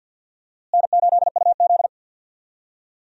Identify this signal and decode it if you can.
Morse code, heard as a keyed tone, about 48 wpm, 700 Hz. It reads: N8UZ
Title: USA Call Signs - 4 Characters - Repeated 3x - 50wpm